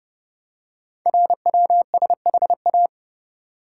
Morse code, heard as a keyed tone, about 30 wpm, 700 Hz. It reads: RWSHA
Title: Code Groups - 5 Letters - 30wpm